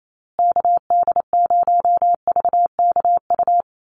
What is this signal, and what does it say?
Morse code, heard as a keyed tone, about 28 wpm, 700 Hz. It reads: KD0VKU
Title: USA Call Signs Repeated 2x 28wpm